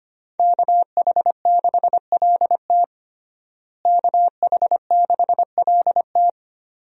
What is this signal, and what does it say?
Morse code, heard as a keyed tone, about 25 wpm, 700 Hz. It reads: KH6LT KH6LT
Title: USA Call Signs - 5 Characters - Repeated 2x - 25wpm